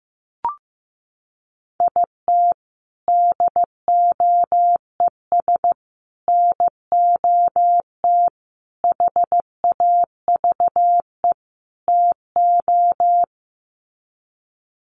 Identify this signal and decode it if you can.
Morse code, heard as a keyed tone, about 15 wpm, 700 Hz. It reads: IT DOES NOT HAVE TO
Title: Sets of 5 Words 15wpm